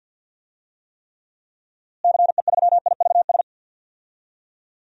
Morse code, heard as a keyed tone, about 47 wpm, 700 Hz. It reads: CE3IVS
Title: DXCC-100 Call Signs - 6 Characters - Repeated 2x - 50wpm